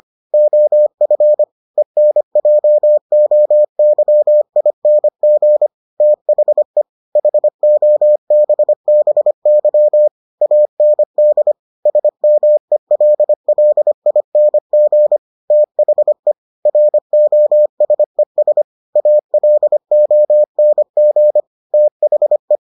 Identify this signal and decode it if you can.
Morse code, heard as a keyed tone, about 25 wpm, 600 Hz. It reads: OF ENJOYING THE HOBBY AND SMELLING THE ROSES ALONG THE